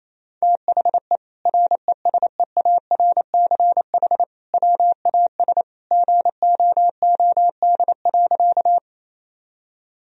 Morse code, heard as a keyed tone, about 27 wpm, 700 Hz. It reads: THE RESEARCH WAS GOOD.
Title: Sentences from Top 1000 Words - Review Part 4 - 28wpm